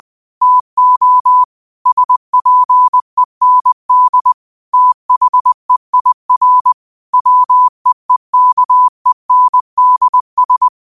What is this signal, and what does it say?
Morse code, heard as a keyed tone, about 20 wpm, 1000 Hz. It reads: TO SPEND THEIR WEEKENDS